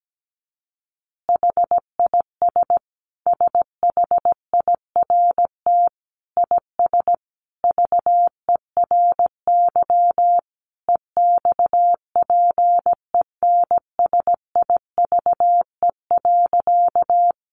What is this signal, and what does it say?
Morse code, heard as a keyed tone, about 17 wpm, 700 Hz. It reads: HIS SHIRT IS VERY EXPENSIVE.
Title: Additional Easy Phrases 17wpm